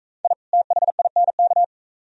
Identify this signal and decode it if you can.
Morse code, heard as a keyed tone, about 37 wpm, 700 Hz. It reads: I THINK